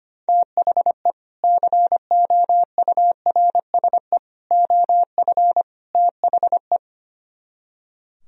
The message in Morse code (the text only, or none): THE COURSE OF THE